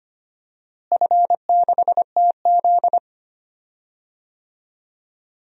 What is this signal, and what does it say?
Morse code, heard as a keyed tone, about 25 wpm, 700 Hz. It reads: F6TZ
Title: DXCC-100 Call Signs - 4 Characters - Repeated 2x - 25wpm